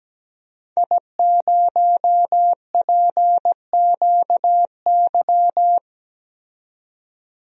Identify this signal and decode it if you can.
Morse code, heard as a keyed tone, about 17 wpm, 700 Hz. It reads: I0PQY
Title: DXCC-100 Call Signs - 5 Characters - 17wpm